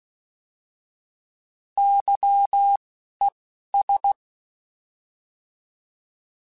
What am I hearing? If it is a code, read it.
Morse code, heard as a keyed tone, about 16 wpm, 800 Hz. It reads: Y E S
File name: Misc / Misc-mp3s / Misc @10